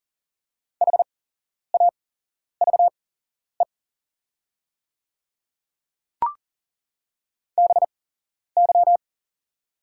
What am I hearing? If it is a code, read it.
Morse code, heard as a keyed tone, about 40 wpm, spaced as for 11 wpm, 700 Hz. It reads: HAVE BY
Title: 8x Character Spacing - Top 100 Words - 40wpm